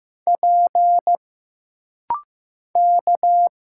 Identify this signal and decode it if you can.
Morse code, heard as a keyed tone, about 15 wpm, 700 Hz. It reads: P K
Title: Single Letter-Number Alt - Rapid-Fire 15wpm